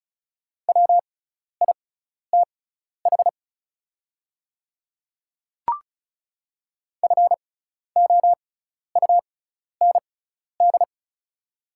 Morse code, heard as a keyed tone, about 34 wpm, spaced as for 12 wpm, 700 Hz. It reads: WITH FOUND
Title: 6x Character Spacing - Top 300 Words - 35wpm